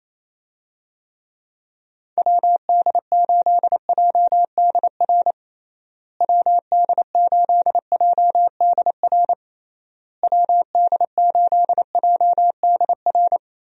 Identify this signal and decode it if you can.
Morse code, heard as a keyed tone, about 28 wpm, 700 Hz. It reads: WD8JDR WD8JDR WD8JDR
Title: USA Call Signs - 6 Characters - Repeated 3x - 28wpm